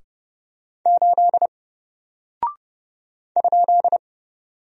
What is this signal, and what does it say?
Morse code, heard as a keyed tone, about 30 wpm, 700 Hz. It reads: 8 ?
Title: Single Character (Letter:Number:Punctuation:Prosign) 30wpm